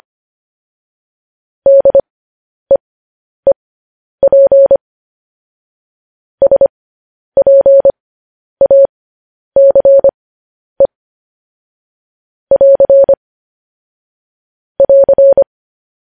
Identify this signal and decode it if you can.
Morse code, heard as a keyed tone, about 25 wpm, spaced as for 10 wpm, 550 Hz. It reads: DEEP SPACE <AR> <AR>